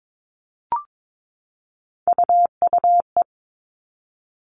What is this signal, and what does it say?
Morse code, heard as a keyed tone, about 22 wpm, 700 Hz. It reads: UUE
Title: All Cut Numbers - 3-Digit 22wpm